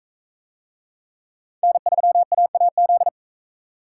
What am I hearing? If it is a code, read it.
Morse code, heard as a keyed tone, about 40 wpm, 700 Hz. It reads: N3AAZ